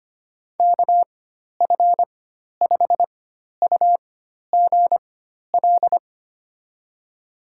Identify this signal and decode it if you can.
Morse code, heard as a keyed tone, about 25 wpm, spaced as for 12 wpm, 700 Hz. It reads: KF5UGL
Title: USA Call Signs - 4x Character Spacing - 25wpm